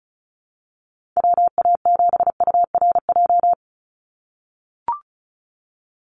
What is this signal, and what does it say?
Morse code, heard as a keyed tone, about 34 wpm, 700 Hz. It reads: WA7URJ
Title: USA Call Signs 35wpm